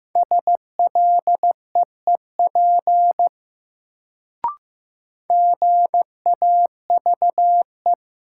Morse code, heard as a keyed tone, about 15 wpm, 700 Hz. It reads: SLEEP GAVE
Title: Sentences from Top 500 Words - New Word Drill - 15wpm